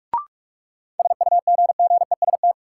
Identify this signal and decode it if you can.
Morse code, heard as a keyed tone, about 42 wpm, 700 Hz. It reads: SUGGEST
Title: Sentences from Top 1300 Words - New Word Drill - 45wpm